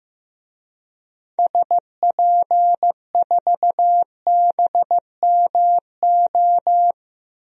Morse code, heard as a keyed tone, about 15 wpm, 700 Hz. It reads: SP4BMO
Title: DXCC-100 Call Signs - 6 Characters - Repeated 3x - 15wpm